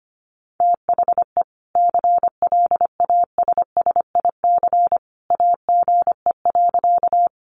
Morse code, heard as a keyed tone, about 25 wpm, 700 Hz. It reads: THE CLASSIC AGE.
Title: Sentences from Top 1000 Words 25wpm